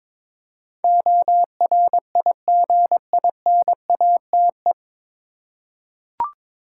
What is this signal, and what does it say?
Morse code, heard as a keyed tone, about 22 wpm, 700 Hz. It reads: ORIGINATE
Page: Morse Code Ninja practice 22wpm